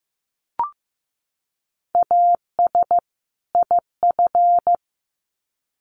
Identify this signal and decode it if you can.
Morse code, heard as a keyed tone, about 15 wpm, 700 Hz. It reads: AS IF
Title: Sets of 2 Words 15wpm